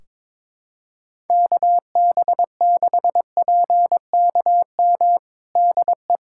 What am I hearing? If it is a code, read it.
Morse code, heard as a keyed tone, about 22 wpm, 700 Hz. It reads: KB6PKM DE
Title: Practice QSO - Exchange 4 - Ending - 22wpm